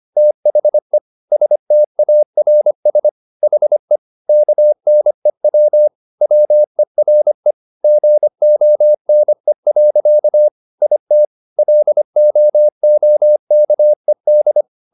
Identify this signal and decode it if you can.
Morse code, heard as a keyed tone, about 25 wpm, 600 Hz. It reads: THE STARS HE KNEW WERE GONE. IT LOOKED